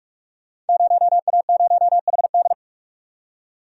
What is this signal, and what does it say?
Morse code, heard as a keyed tone, about 44 wpm, 700 Hz. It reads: 0A0HD
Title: Code Groups - 5 Letter:Numbers - 45wpm